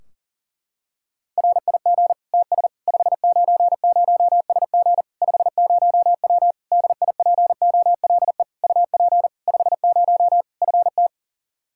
Morse code, heard as a keyed tone, about 37 wpm, 700 Hz. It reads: RIG TS 590SG 50W DIPOLE UP 50 FT
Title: Practice QSO - Exchange 2 - Info - 40wpm